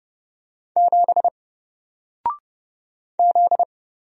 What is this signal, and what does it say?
Morse code, heard as a keyed tone, about 30 wpm, 700 Hz. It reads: 7 Z
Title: Single Character (Letter:Number:Punctuation:Prosign) Alt 30wpm